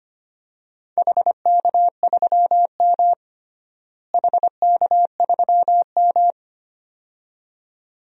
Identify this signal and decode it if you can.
Morse code, heard as a keyed tone, about 25 wpm, 700 Hz. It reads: HK3M HK3M
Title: DXCC-100 Call Signs - 4 Characters - Repeated 2x - 25wpm